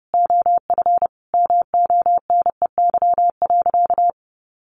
Morse code, heard as a keyed tone, about 30 wpm, 700 Hz. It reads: OF MONEY.
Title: Sentences from Top 1300 Words 30wpm